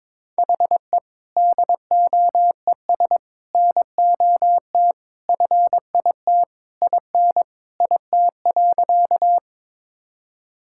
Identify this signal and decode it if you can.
Morse code, heard as a keyed tone, about 22 wpm, 700 Hz. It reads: HE DOES NOT FIT IN IT.
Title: Sentences from Top 1000 Words - Review Part 5 - 22wpm